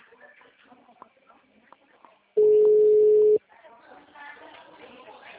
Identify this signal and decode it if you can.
Ringback tone (Europe, 425 Hz), with speech.
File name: out- -2005- - - .44990